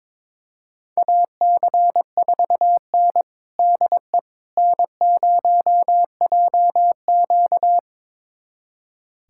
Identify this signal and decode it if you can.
Morse code, heard as a keyed tone, about 22 wpm, 700 Hz. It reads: AC4N DE N0JQ
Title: USA Callsign DE Callsign - 4 Characters - 22wpm